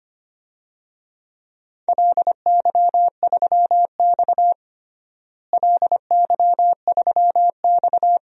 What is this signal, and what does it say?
Morse code, heard as a keyed tone, about 25 wpm, 700 Hz. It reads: LY3X LY3X
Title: DXCC-100 Call Signs - 4 Characters - Repeated 3x - 25wpm